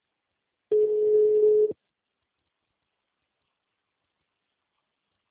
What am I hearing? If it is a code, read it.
Ringback tone (Europe, 425 Hz).